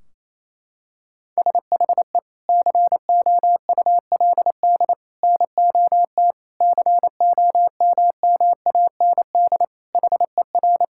Morse code, heard as a keyed tone, about 28 wpm, 700 Hz. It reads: SHE COULD NOT COMMAND HER
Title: Sentences from Top 1600 Words 28wpm